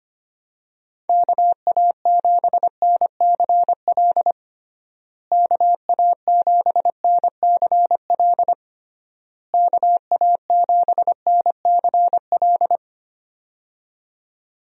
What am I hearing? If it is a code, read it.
Morse code, heard as a keyed tone, about 25 wpm, 700 Hz. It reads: KA7NCL KA7NCL KA7NCL
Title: USA Call Signs - 6 Characters - Repeated 3x - 25wpm